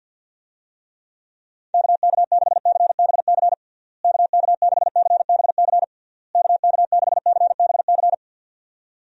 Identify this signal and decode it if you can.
Morse code, heard as a keyed tone, about 51 wpm, 700 Hz. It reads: KK6CBC KK6CBC KK6CBC
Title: USA Call Signs - 6 Characters - Repeated 3x - 50wpm